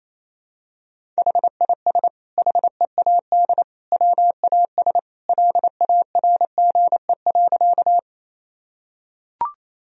Morse code, heard as a keyed tone, about 28 wpm, 700 Hz. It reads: HIS HEAD WAS LARGE.
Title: Sentences from Top 1000 Words - Review Part 1 - 28wpm